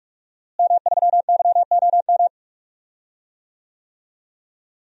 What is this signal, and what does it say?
Morse code, heard as a keyed tone, about 41 wpm, 700 Hz. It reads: M3YOM
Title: DXCC-100 Call Signs - 5 Characters - Repeated 3x - 45wpm